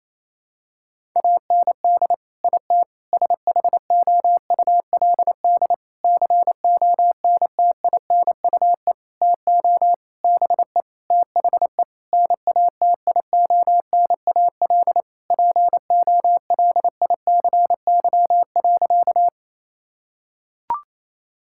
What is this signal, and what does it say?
Morse code, heard as a keyed tone, about 28 wpm, 700 Hz. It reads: AND IT SHOULD CONTINUE TO BE THE NATIONAL POLICY.
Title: Sentences from Top 1000 Words - Review Part 6 - 28wpm